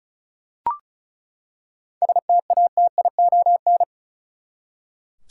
Morse code, heard as a keyed tone, about 34 wpm, 700 Hz. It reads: STATION
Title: Seven Letter Words - 35wpm